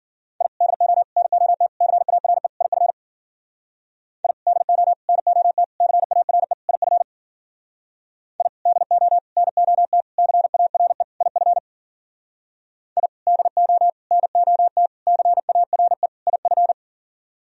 Morse code, heard as a keyed tone, about 51 wpm, 700 Hz. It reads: I DO NOT CARE IF I DO NOT CARE IF I DO NOT CARE IF I DO NOT CARE IF